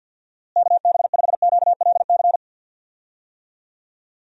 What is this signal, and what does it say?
Morse code, heard as a keyed tone, about 52 wpm, 700 Hz. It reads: KB5QLC
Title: USA Call Signs - 6 Characters - Repeated 2x - 50wpm